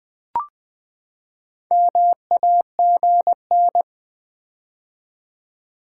Morse code, heard as a keyed tone, about 20 wpm, 700 Hz. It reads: MAGN
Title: Build A Word - 4 Letters At A Time - 11-Letter Words - 20wpm